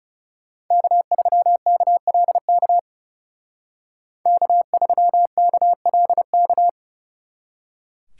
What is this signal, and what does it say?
Morse code, heard as a keyed tone, about 39 wpm, 700 Hz. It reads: K3KLK K3KLK